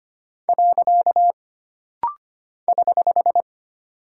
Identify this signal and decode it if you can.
Morse code, heard as a keyed tone, about 25 wpm, 700 Hz. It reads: . <HH>
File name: Single Punctuation-Prosign - Rapid-Fire - 25wpm